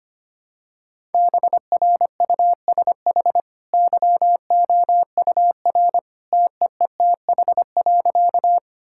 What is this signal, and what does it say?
Morse code, heard as a keyed tone, about 25 wpm, 700 Hz. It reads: BRUSH YOUR TEETH.
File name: Easy Sentences with Hints - 25wpm